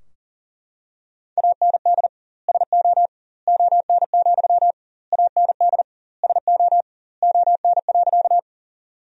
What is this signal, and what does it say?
Morse code, heard as a keyed tone, about 40 wpm, 700 Hz. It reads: AND SO ON, AND SO ON.
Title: 2x Word-Spacing - Sentences from Top 100 Words - 40wpm